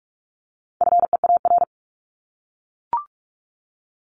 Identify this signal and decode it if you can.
Morse code, heard as a keyed tone, about 44 wpm, 700 Hz. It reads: FEAR
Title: Top 1751-2000 Words 45wpm